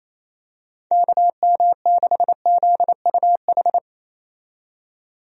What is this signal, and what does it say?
Morse code, heard as a keyed tone, about 28 wpm, 700 Hz. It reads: KM6ZUH
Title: USA Call Signs - 6 Characters - 28wpm